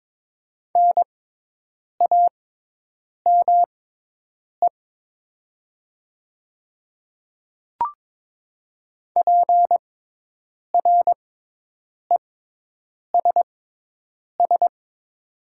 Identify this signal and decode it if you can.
Morse code, heard as a keyed tone, about 22 wpm, spaced as for 8 wpm, 700 Hz. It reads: NAME PRESS
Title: 6x Character Spacing - Top 500 Words - 22wpm